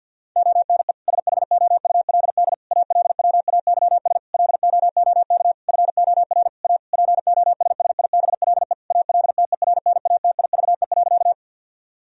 Morse code, heard as a keyed tone, about 51 wpm, 700 Hz. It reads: ONE SHOULD ALWAYS LOOK FOR A POSSIBLE ALTERNATIVE.
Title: Sentences from Top 1000 Words - Review Part 4 - 50wpm